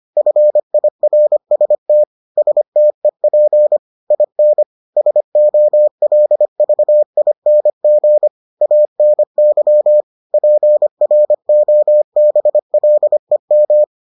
Morse code, heard as a keyed tone, about 25 wpm, 600 Hz. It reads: FIRST STEP IN SOLVING ANY PROBLEM